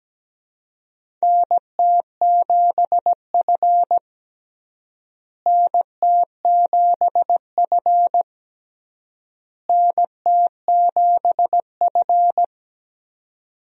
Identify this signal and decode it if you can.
Morse code, heard as a keyed tone, about 17 wpm, 700 Hz. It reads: NT7F NT7F NT7F
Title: USA Call Signs - 4 Characters - Repeated 3x - 17wpm